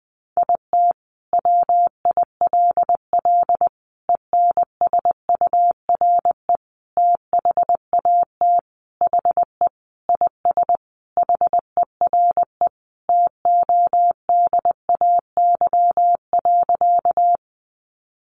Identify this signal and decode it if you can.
Morse code, heard as a keyed tone, about 20 wpm, 700 Hz. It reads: IT WILL ENSURE THAT HE IS HERE TODAY.
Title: Sentences from Top 1100 Words 20wpm